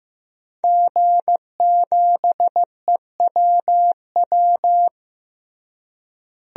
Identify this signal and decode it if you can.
Morse code, heard as a keyed tone, about 15 wpm, 700 Hz. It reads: G7EWW
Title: DXCC-100 Call Signs - 5 Characters - Repeated 2x - 15wpm